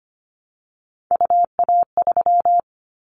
Morse code, heard as a keyed tone, about 25 wpm, 700 Hz. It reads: UA3
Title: Callsign Prefixes for Top 100 DXCC - 25wpm